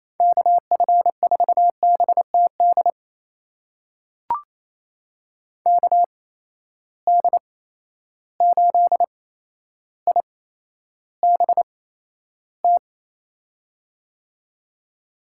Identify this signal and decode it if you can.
Morse code, heard as a keyed tone, about 28 wpm, spaced as for 12 wpm, 700 Hz. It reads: KF4BTD KD8IBT
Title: USA Call Signs - 6 Characters - 8x Character Spacing - 28wpm